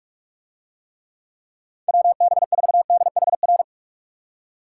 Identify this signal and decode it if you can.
Morse code, heard as a keyed tone, about 47 wpm, 700 Hz. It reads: WB4DHR